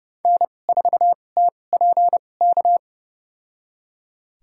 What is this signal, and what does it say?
Morse code, heard as a keyed tone, about 30 wpm, spaced as for 22 wpm, 700 Hz. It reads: N4TPK
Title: USA Call Signs - 2x Character Spacing - 30wpm